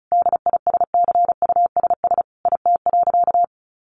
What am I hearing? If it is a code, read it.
Morse code, heard as a keyed tone, about 34 wpm, 700 Hz. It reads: DISCUSS IT.
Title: Sentences from Top 1400 Words 35wpm